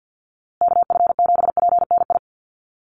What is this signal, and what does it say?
Morse code, heard as a keyed tone, about 48 wpm, 700 Hz. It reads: KF7PNI
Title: USA Call Signs Repeated 3x 50wpm